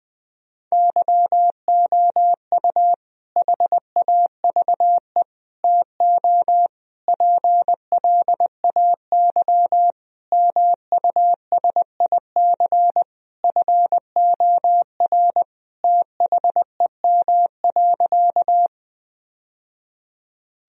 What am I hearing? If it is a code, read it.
Morse code, heard as a keyed tone, about 20 wpm, 700 Hz. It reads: YOU HAVE TO PLAY MUSIC FOR THEM.
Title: Sentences from Top 1000 Words - Review Part 1 - 20wpm